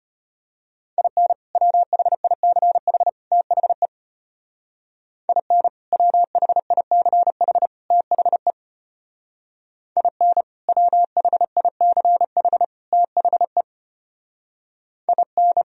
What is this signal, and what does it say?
Morse code, heard as a keyed tone, about 33 wpm, 700 Hz. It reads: IN WHICH THE IN WHICH THE IN WHICH THE IN